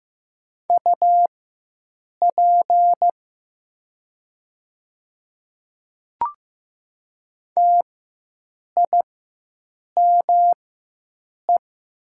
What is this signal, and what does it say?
Morse code, heard as a keyed tone, about 15 wpm, spaced as for 7 wpm, 700 Hz. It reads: UP TIME
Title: 4x Character Spacing - Top 100 Words - 15wpm